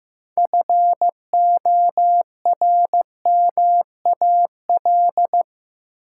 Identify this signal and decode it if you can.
Morse code, heard as a keyed tone, about 15 wpm, 700 Hz. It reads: FORMAL
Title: Six Letter Words - 15wpm